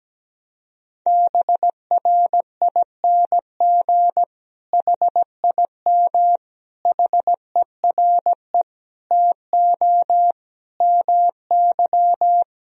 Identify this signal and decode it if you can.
Morse code, heard as a keyed tone, about 17 wpm, 700 Hz. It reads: BRING HIM HERE TO MY
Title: Sentences from Top 1000 Words - Review Part 3 - 17wpm